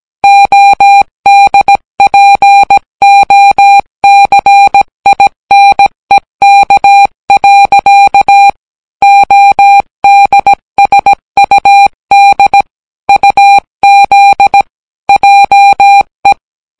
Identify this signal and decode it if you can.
Morse code, heard as a keyed tone, about 17 wpm, 800 Hz. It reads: ODPOCINEK. ODSUD UZ JE